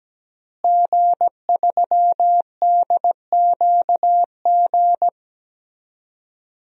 Morse code, heard as a keyed tone, about 17 wpm, 700 Hz. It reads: G3DQG
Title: DXCC-100 Call Signs - 5 Characters - Repeated 3x - 17wpm